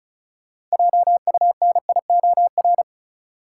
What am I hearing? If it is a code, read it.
Morse code, heard as a keyed tone, about 36 wpm, 700 Hz. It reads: JUNIOR
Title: Sentences from Top 1500 Words - New Word Drill Alt - 35wpm